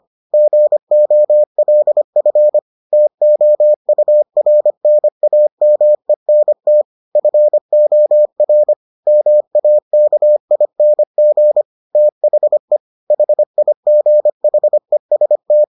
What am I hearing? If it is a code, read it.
Morse code, heard as a keyed tone, about 25 wpm, 600 Hz. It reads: GOLF TOURNAMENT FOR MAKING THE HIGHEST